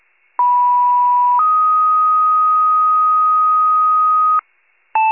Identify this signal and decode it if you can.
Two-tone page, heard as a one-second tone, then a three-second tone, about 980 then 1250 Hz. It is heard from 0.4 to 4.4 s.